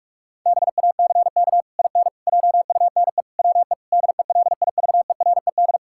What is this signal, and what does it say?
Morse code, heard as a keyed tone, about 41 wpm, 700 Hz. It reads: BACK IN JUNE WE DELIVERED
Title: Pangram Challenge v2 - 45wpm